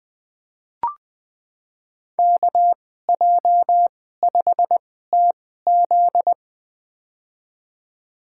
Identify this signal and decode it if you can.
Morse code, heard as a keyed tone, about 21 wpm, spaced as for 15 wpm, 700 Hz. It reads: KJ5TZ
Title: USA Call Signs - 5 Characters - 2x Character Spacing - 20wpm